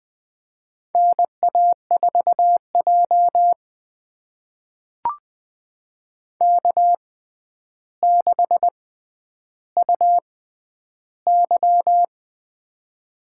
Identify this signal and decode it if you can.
Morse code, heard as a keyed tone, about 21 wpm, spaced as for 10 wpm, 700 Hz. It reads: NA4J K6UY
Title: USA Call Signs - 4 Characters - 6x Character Spacing - 20wpm